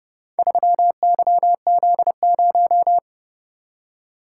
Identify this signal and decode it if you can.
Morse code, heard as a keyed tone, about 30 wpm, 700 Hz. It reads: 3YZ0